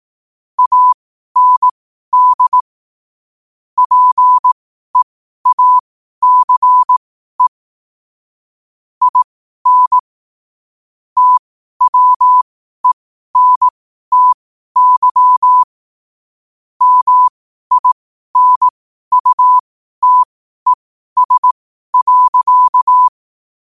Morse code, heard as a keyed tone, about 18 wpm, spaced as for 12 wpm, 1000 Hz. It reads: AND PEACE IN TWENTY MINUTES.
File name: qotdfiles / 12wpm